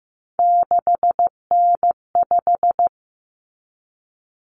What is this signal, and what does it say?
Morse code, heard as a keyed tone, about 15 wpm, 700 Hz. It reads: 6N5
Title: Common Cut Numbers - 3-Digit 15wpm